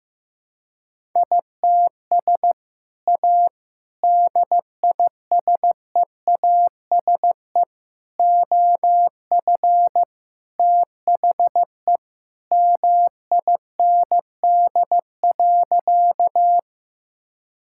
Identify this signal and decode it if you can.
Morse code, heard as a keyed tone, about 16 wpm, 700 Hz. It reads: ITS A DISEASE OF THE MIND.
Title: Sentences from Top 1000 Words - Review Part 6 - 15wpm